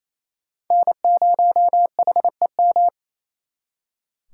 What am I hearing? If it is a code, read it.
Morse code, heard as a keyed tone, about 38 wpm, 700 Hz. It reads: N0HEM